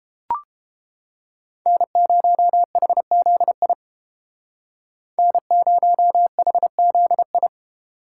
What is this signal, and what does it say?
Morse code, heard as a keyed tone, about 31 wpm, 700 Hz. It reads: N0HZI N0HZI